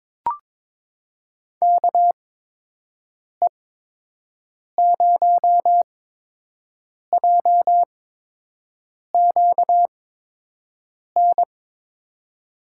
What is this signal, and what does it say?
Morse code, heard as a keyed tone, about 22 wpm, spaced as for 6 wpm, 700 Hz. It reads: KE0JQN
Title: USA Call Signs - 8x Character Spacing - 22wpm